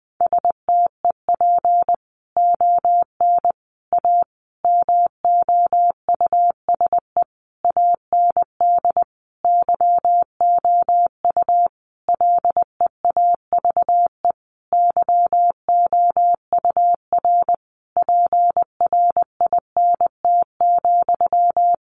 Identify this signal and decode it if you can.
Morse code, heard as a keyed tone, about 20 wpm, 700 Hz. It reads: STEP ON A MOUSE AND YOU LEAVE YOUR PRINT,
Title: A Sound Of Thunder 20wpm